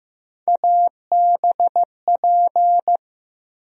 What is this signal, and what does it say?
Morse code, heard as a keyed tone, about 15 wpm, 700 Hz. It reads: ABP
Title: USA Call Sign Postfixes - 3 Letters 15wpm